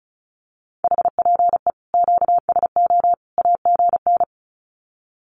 Morse code, heard as a keyed tone, about 34 wpm, 700 Hz. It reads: HPE QSO AGN
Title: Common QSO Phrases 35wpm